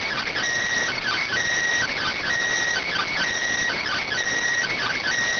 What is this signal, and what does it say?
Emergency Alert System (SAME) tones, heard as harsh buzzing bursts. It is heard from 0.9 to 5.1 s.